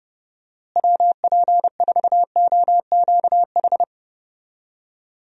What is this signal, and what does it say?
Morse code, heard as a keyed tone, about 32 wpm, 700 Hz. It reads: WP4OQH
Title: USA Call Signs - 6 Characters - Repeated 2x - 30wpm